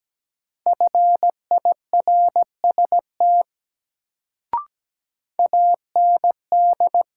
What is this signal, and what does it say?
Morse code, heard as a keyed tone, about 17 wpm, 700 Hz. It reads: FIRST AND
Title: Sentences from Top 100 Words - New Word Drill - 17wpm